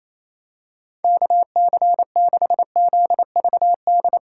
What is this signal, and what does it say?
Morse code, heard as a keyed tone, about 27 wpm, 700 Hz. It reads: KC6ZVD
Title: USA Call Signs - 8x Character Spacing - 28wpm